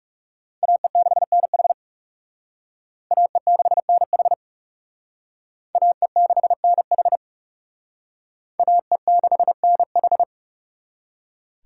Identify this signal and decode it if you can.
Morse code, heard as a keyed tone, about 40 wpm, 700 Hz. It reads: AE6NH AE6NH AE6NH AE6NH